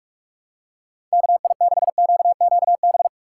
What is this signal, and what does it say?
Morse code, heard as a keyed tone, about 43 wpm, 700 Hz. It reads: KI6QQB
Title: USA Call Signs - 6 Characters - 4x Character Spacing - 45wpm